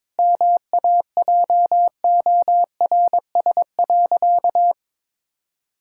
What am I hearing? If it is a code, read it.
Morse code, heard as a keyed tone, about 22 wpm, 700 Hz. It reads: MAJORS.
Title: Pangram Challenge v2 - 22wpm